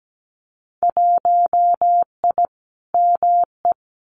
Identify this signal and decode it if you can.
Morse code, heard as a keyed tone, about 17 wpm, 700 Hz. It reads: 1I ME
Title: Winter Field Day Exchanges 17wpm